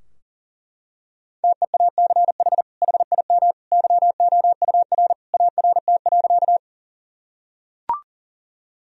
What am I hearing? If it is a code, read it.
Morse code, heard as a keyed tone, about 40 wpm, 700 Hz. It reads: TEACH HIM YOUR ART.
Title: Sentences from Top 1000 Words - Review Part 3 - 40wpm